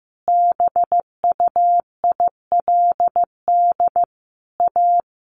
Morse code, heard as a keyed tone, about 15 wpm, 700 Hz. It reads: BUILD A
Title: Sentences from Top 1100 Words 15wpm